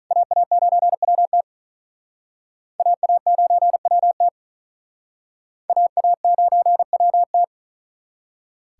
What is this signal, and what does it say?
Morse code, heard as a keyed tone, about 43 wpm, 700 Hz. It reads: AA9WT AA9WT AA9WT
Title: Speed Racing - USA 2x2 Call Signs 35wpm